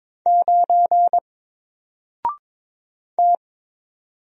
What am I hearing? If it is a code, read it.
Morse code, heard as a keyed tone, about 22 wpm, 700 Hz. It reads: 9 T
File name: Single-Letter-Number-22wpm